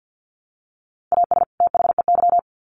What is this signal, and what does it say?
Morse code, heard as a keyed tone, about 50 wpm, 700 Hz. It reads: AS THEY
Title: Sets of 2 Words 50wpm